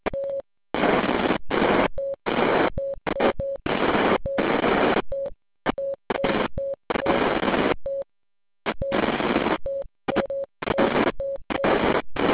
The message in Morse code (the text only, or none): none